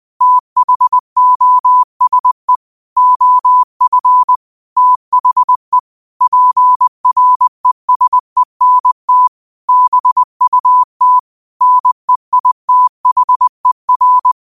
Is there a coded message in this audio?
Yes: Morse code from 0.2 to 14.3 s, heard as a keyed tone, about 20 wpm, 1000 Hz. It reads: THOSE OF THE PRESENT BUT NEITHER